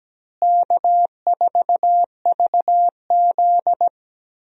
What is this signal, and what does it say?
Morse code, heard as a keyed tone, about 17 wpm, 700 Hz. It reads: K4VZ
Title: USA Call Signs - 4 Characters - Repeated 3x - 17wpm